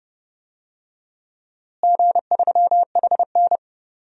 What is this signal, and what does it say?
Morse code, heard as a keyed tone, about 30 wpm, 700 Hz. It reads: G3HN
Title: DXCC-100 Call Signs - 4 Characters - Repeated 3x - 30wpm